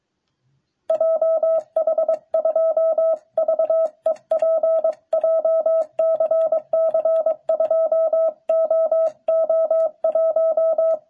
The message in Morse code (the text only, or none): JH2VEP JCC 2OO1